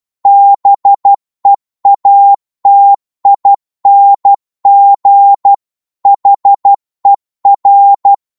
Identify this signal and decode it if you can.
Morse code, heard as a keyed tone, about 12 wpm, 800 Hz. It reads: BEATING HER